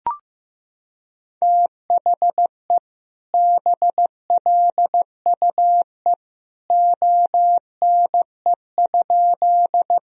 Morse code, heard as a keyed tone, about 15 wpm, 700 Hz. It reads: THE BLUE ONE?
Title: Sentences from Top 1000 Words - Review Part 3 - 15wpm